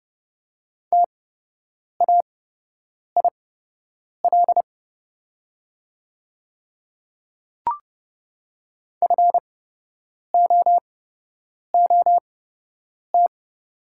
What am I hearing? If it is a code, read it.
Morse code, heard as a keyed tone, about 32 wpm, spaced as for 8 wpm, 700 Hz. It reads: TAIL FOOT
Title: 8x Character Spacing - Top 500 Words - 30wpm